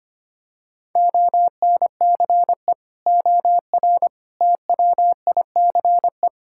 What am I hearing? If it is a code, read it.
Morse code, heard as a keyed tone, about 25 wpm, 700 Hz. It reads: ONCE OR TWICE
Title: Sentences from Top 1600 Words 25wpm